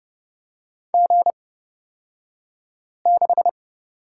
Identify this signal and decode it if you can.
Morse code, heard as a keyed tone, about 30 wpm, spaced as for 5 wpm, 700 Hz. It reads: G6
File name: Lesson 38d - Letter q - Two Characters 30wpm